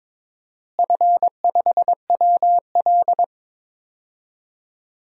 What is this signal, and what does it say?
Morse code, heard as a keyed tone, about 22 wpm, 700 Hz. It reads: F5WL
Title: DXCC-100 Call Signs - 4 Characters - Repeated 2x - 22wpm